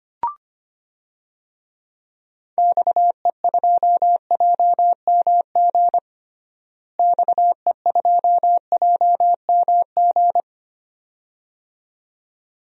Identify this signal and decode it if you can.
Morse code, heard as a keyed tone, about 25 wpm, 700 Hz. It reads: XE2JMG XE2JMG
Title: DXCC-100 Call Signs - 6 Characters - Repeated 2x - 25wpm